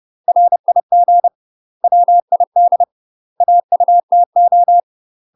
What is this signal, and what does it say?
Morse code, heard as a keyed tone, about 30 wpm, 700 Hz. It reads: RIG WID AUTO
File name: UK MASTER QSOs TRAINING / 30WPM / UK MASTER QSOs 0033